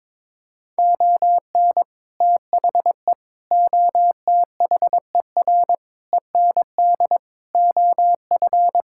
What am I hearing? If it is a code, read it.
Morse code, heard as a keyed tone, about 22 wpm, 700 Hz. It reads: ON THE OTHER END OF
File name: practice / 22wpm / Sets of 7 Words - 22wpm